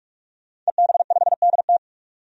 Morse code, heard as a keyed tone, about 43 wpm, 700 Hz. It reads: EB5DT
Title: DXCC-100 Call Signs - 5 Characters - 45wpm